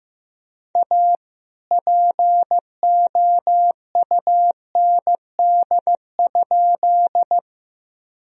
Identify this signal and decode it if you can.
Morse code, heard as a keyed tone, about 15 wpm, 700 Hz. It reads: A POUND?
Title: Sentences from Top 1000 Words - Review Part 2 - 15wpm